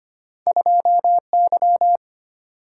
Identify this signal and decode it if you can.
Morse code, heard as a keyed tone, about 25 wpm, 700 Hz. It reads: 2Y
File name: Two Letter-Number Combinations - 25wpm